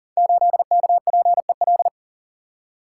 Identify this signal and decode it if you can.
Morse code, heard as a keyed tone, about 42 wpm, 700 Hz. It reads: 8KPEL
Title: Code Groups - 5 Letter:Numbers - 40wpm